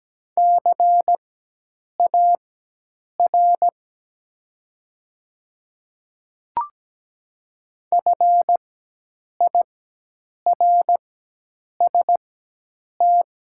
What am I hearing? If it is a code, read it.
Morse code, heard as a keyed tone, about 17 wpm, spaced as for 8 wpm, 700 Hz. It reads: CAR FIRST
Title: 4x Character Spacing - Top 300 Words - 17wpm